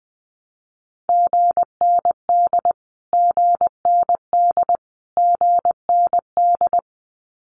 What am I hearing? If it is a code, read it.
Morse code, heard as a keyed tone, about 20 wpm, 700 Hz. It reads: GND GND GND
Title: Abbreviations Repeated 3x - 20wpm